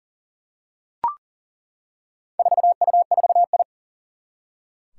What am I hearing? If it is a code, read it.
Morse code, heard as a keyed tone, about 40 wpm, 700 Hz. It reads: 4U4I
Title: Four Challenging Letter-Number Combinations 40wpm